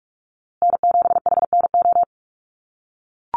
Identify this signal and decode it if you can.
Morse code, heard as a keyed tone, about 44 wpm, 690 Hz. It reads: N7HNO
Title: USA Call Signs Repeated 2x 45wpm